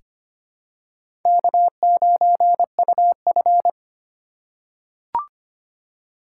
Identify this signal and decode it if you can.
Morse code, heard as a keyed tone, about 25 wpm, 690 Hz. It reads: K9UF
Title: USA Call Signs - 4 Characters - 4x Character Spacing - 25wpm